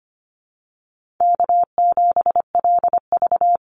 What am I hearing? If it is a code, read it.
Morse code, heard as a keyed tone, about 25 wpm, 700 Hz. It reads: K7LV
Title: Easy Call Signs - 25wpm